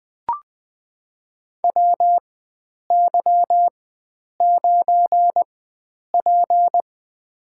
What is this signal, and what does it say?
Morse code, heard as a keyed tone, about 20 wpm, spaced as for 9 wpm, 700 Hz. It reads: WY9P
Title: USA Call Signs - 4 Characters - 4x Character Spacing - 20wpm